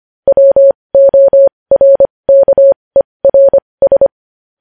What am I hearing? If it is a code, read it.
Morse code, heard as a keyed tone, about 25 wpm, spaced as for 20 wpm, 550 Hz. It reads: WORKERS
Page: News Headlines in Morse Code at 20 WPM